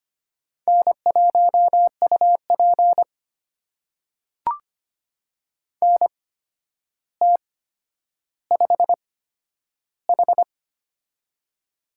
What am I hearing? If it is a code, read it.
Morse code, heard as a keyed tone, about 25 wpm, spaced as for 11 wpm, 700 Hz. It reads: N1UP NT5H
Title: USA Call Signs - 8x Character Spacing - 25wpm